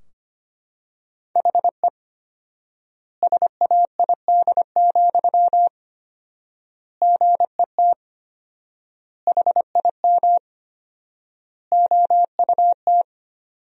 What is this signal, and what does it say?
Morse code, heard as a keyed tone, about 25 wpm, 700 Hz. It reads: HE SAID, GET HIM OUT
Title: 4x Word-Spacing - Sentences from Top 400 Words - 25wpm